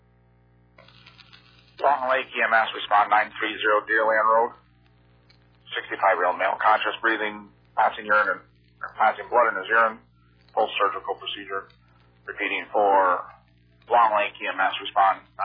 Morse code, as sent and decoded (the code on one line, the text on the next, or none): none